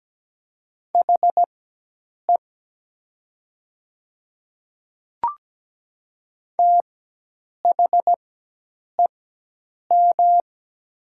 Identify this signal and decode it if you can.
Morse code, heard as a keyed tone, about 17 wpm, spaced as for 8 wpm, 700 Hz. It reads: HE THEM
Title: 4x Character Spacing - Top 100 Words - 17wpm